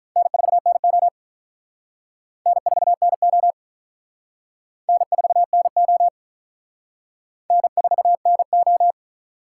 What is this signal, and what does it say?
Morse code, heard as a keyed tone, about 42 wpm, 700 Hz. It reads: N4NO N4NO N4NO N4NO
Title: Speed Racing - USA 1x2 Call Signs 35wpm